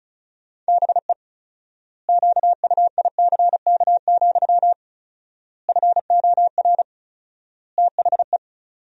Morse code, heard as a keyed tone, about 36 wpm, 700 Hz. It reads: BE QUICK, FOR THE
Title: 4x Word-Spacing - Sentences from Top 500 Words - 35wpm